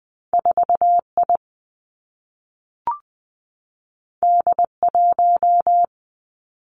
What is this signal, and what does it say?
Morse code, heard as a keyed tone, about 20 wpm, 700 Hz. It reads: 4 I D 1
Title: Two Challenging Letter-Number Combinations 20wpm